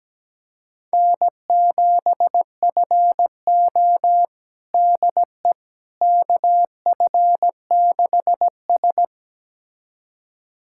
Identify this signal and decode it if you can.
Morse code, heard as a keyed tone, about 17 wpm, 700 Hz. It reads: N7FO DE KF6S
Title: USA Callsign DE Callsign - 4 Characters - 17wpm